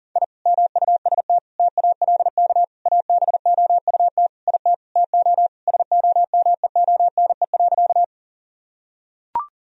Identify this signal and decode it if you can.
Morse code, heard as a keyed tone, about 40 wpm, 700 Hz. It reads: I MUST TALK ABOUT IT TO SOMEONE.
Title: Sentences from Top 1000 Words - Review Part 4 - 40wpm